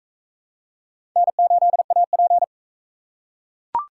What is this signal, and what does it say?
Morse code, heard as a keyed tone, about 40 wpm, 700 Hz. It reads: N8AP